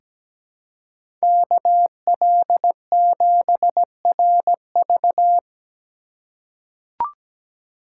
Morse code, heard as a keyed tone, about 17 wpm, 700 Hz. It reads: KL7RV
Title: USA Call Signs - 5 Characters - 2x Character Spacing - 17wpm